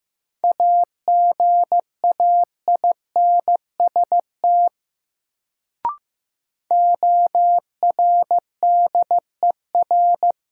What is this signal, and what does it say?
Morse code, heard as a keyed tone, about 15 wpm, 700 Hz. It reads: AGAINST ORDER
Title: Sentences from Top 400 Words - New Word Drill - 15wpm